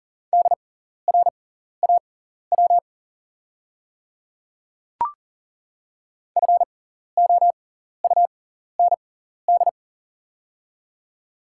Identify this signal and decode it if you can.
Morse code, heard as a keyed tone, about 40 wpm, spaced as for 14 wpm, 700 Hz. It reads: DRAW FOUND
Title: 6x Character Spacing - Top 300 Words - 40wpm